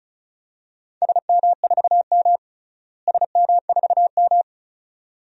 Morse code, heard as a keyed tone, about 36 wpm, 700 Hz. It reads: SM4M SM4M
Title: DXCC-100 Call Signs - 4 Characters - Repeated 2x - 35wpm